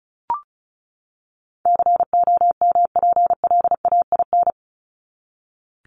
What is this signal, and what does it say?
Morse code, heard as a keyed tone, about 34 wpm, 700 Hz. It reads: COMPLAIN
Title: Top 1001-1250 Words 35wpm